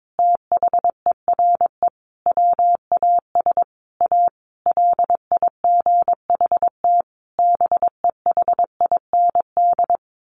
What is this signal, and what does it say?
Morse code, heard as a keyed tone, about 22 wpm, 700 Hz. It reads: THERE WAS A LIGHT BEHIND
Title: Sentences from Top 1000 Words 22wpm